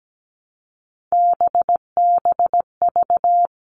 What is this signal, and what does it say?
Morse code, heard as a keyed tone, about 17 wpm, 700 Hz. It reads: BBV
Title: All Cut Numbers - 3-Digit 17wpm